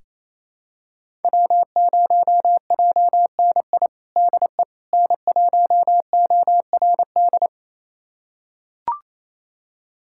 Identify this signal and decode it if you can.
Morse code, heard as a keyed tone, about 28 wpm, 700 Hz. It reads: W0JNI DE N1ORD
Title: USA Callsign DE Callsign - 5 Characters - 28wpm